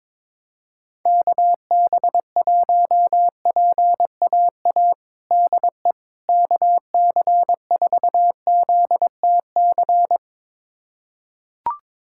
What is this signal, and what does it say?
Morse code, heard as a keyed tone, about 22 wpm, 700 Hz. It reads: KB1PAA DE KC4ZTC
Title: USA Callsign DE Callsign - 6 Characters - 22wpm